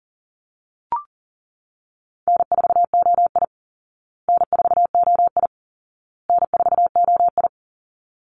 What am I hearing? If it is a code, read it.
Morse code, heard as a keyed tone, about 40 wpm, 700 Hz. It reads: N4OI N4OI N4OI
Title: Easy Call Signs Repeated 3x - 40wpm